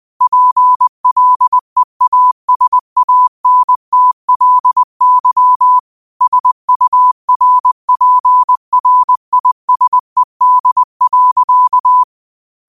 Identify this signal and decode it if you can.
Morse code, heard as a keyed tone, about 20 wpm, 1000 Hz. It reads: PLEASANTLY SURPRISED.
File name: qotdfiles / 20wpm